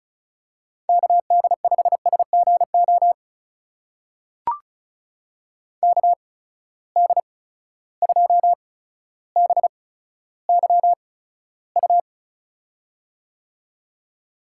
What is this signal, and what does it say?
Morse code, heard as a keyed tone, about 36 wpm, spaced as for 15 wpm, 700 Hz. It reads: KD5SGO KD2BYU
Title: USA Call Signs - 6 Characters - 8x Character Spacing - 35wpm